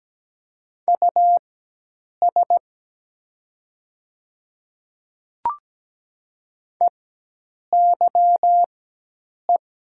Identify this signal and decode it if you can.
Morse code, heard as a keyed tone, about 17 wpm, spaced as for 8 wpm, 700 Hz. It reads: US EYE
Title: 4x Character Spacing - Top 300 Words - 17wpm